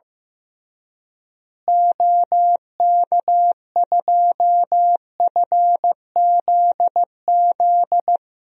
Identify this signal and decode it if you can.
Morse code, heard as a keyed tone, about 15 wpm, 700 Hz. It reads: OK2FZZ
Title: DXCC-100 Call Signs - 6 Characters - Repeated 3x - 15wpm